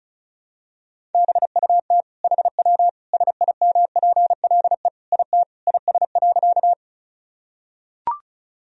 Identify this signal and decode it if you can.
Morse code, heard as a keyed tone, about 36 wpm, 700 Hz. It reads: BUT HW SIMPLE IT IS.
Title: Sentences from Top 400 Words and Abbreviations 35wpm